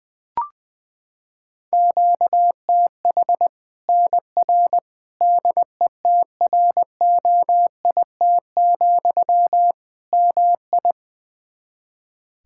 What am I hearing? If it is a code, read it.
Morse code, heard as a keyed tone, about 20 wpm, 700 Hz. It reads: QTH NR DETROIT, MI
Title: Practice QSO - Exchange 1 - Beginning - 20wpm